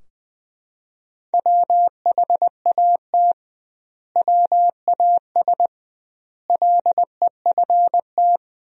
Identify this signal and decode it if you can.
Morse code, heard as a keyed tone, about 21 wpm, 700 Hz. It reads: WHAT WAS LEFT
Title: 2x Word-Spacing - Sentences from Top 300 Words - 20wpm